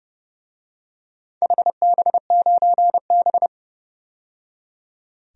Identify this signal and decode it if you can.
Morse code, heard as a keyed tone, about 32 wpm, 700 Hz. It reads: HB9B
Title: DXCC-100 Call Signs - 4 Characters - 30wpm